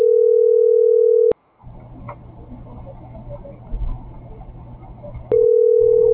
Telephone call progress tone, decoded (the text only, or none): ringback tone